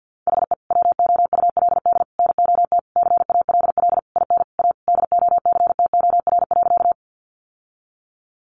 Morse code, heard as a keyed tone, about 50 wpm, 700 Hz. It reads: HE WOULD NOT CALL IN A DOCTOR.
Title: Sentences from Top 1500 Words 50wpm